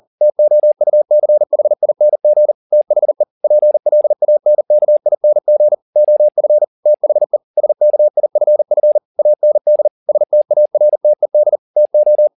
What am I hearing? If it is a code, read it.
Morse code, heard as a keyed tone, about 40 wpm, 600 Hz. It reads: TOUCHING THE PLANKING OF THE SKIFF AND STARTED TO
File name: the old man and the sea 2 40wpm